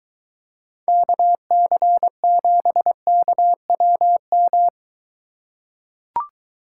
Morse code, heard as a keyed tone, about 23 wpm, 700 Hz. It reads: KC7KWM